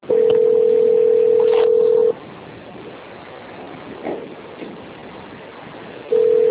Ringback tone (North America), 440 + 480 Hz.
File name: out- - - - - .49919